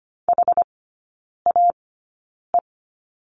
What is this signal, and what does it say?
Morse code, heard as a keyed tone, about 25 wpm, spaced as for 9 wpm, 700 Hz. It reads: HAE
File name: Lesson 17e - Letter u - Three Characters 25wpm